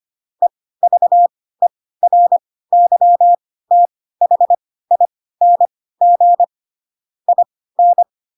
Morse code, heard as a keyed tone, about 25 wpm, spaced as for 16 wpm, 700 Hz. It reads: EVERYTHING IN
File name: Pinocchio / Pinocchio ch7 10 25 ik1hge